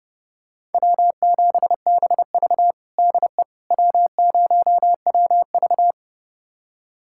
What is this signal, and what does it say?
Morse code, heard as a keyed tone, about 30 wpm, 700 Hz. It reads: W7BV DE W0WV
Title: USA Callsign DE Callsign - 4 Characters - 30wpm